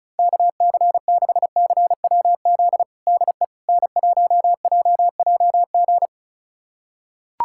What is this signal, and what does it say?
Morse code, heard as a keyed tone, about 36 wpm, 700 Hz. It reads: KC6CWZ DE N1JJG
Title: USA Callsign DE Callsign 35wpm